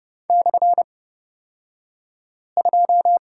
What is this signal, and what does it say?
Morse code, heard as a keyed tone, about 30 wpm, spaced as for 5 wpm, 690 Hz. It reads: /2
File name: Lesson 36d - BT - Two Characters 30wpm